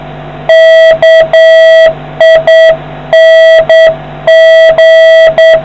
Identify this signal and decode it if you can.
Morse code, heard as a keyed tone, about 8 wpm, 650 Hz, with noise. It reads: KING